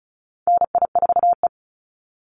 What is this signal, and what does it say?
Morse code, heard as a keyed tone, about 34 wpm, 700 Hz. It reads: NI4E
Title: Call Signs Repeated 3x - 35wpm